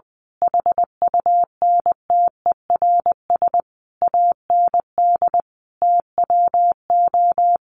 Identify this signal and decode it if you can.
Morse code, heard as a keyed tone, about 20 wpm, 700 Hz. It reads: HUNTERS AND TWO
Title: A Sound Of Thunder 20wpm